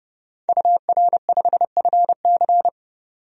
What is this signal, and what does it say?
Morse code, heard as a keyed tone, about 32 wpm, 700 Hz. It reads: UR5FC
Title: DXCC-100 Call Signs - 5 Characters - 30wpm